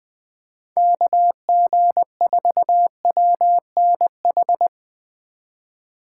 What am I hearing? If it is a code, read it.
Morse code, heard as a keyed tone, about 20 wpm, 700 Hz. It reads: KG4WNH